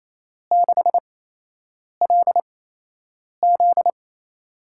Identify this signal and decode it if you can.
Morse code, heard as a keyed tone, about 28 wpm, spaced as for 8 wpm, 700 Hz. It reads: 6LZ
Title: USA Call Signs - 8x Character Spacing - 28wpm